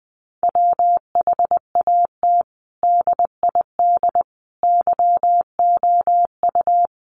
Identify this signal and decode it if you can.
Morse code, heard as a keyed tone, about 20 wpm, 700 Hz. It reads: WHAT DID YOU
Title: Sentences from Top 100 Words 20wpm